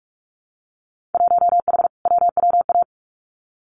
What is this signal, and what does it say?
Morse code, heard as a keyed tone, about 46 wpm, 700 Hz. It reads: 1H WWA
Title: Winter Field Day Exchanges Encore 45wpm